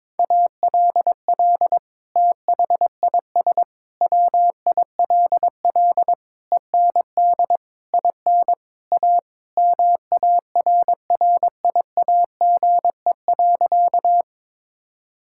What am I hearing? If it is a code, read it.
Morse code, heard as a keyed tone, about 22 wpm, 700 Hz. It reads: ALL THIS WILL END IN A MARRIAGE.
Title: Sentences from Top 1000 Words - Review Part 5 - 22wpm